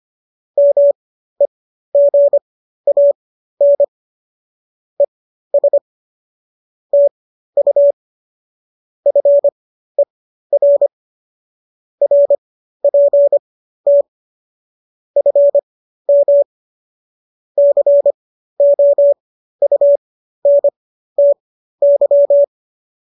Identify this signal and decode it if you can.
Morse code, heard as a keyed tone, about 25 wpm, spaced as for 13 wpm, 570 Hz. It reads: MEGAN ES TU FER RPT FM COUNTY